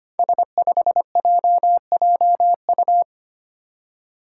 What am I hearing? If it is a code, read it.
Morse code, heard as a keyed tone, about 25 wpm, 700 Hz. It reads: S5JJU
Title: Five Challenging Letter-Number Combinations 25wpm